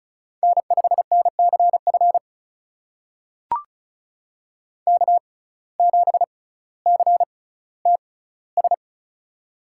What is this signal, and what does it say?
Morse code, heard as a keyed tone, about 36 wpm, spaced as for 18 wpm, 700 Hz. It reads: N5NCF K7CTS
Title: USA Call Signs - 5 Characters - 6x Character Spacing - 35wpm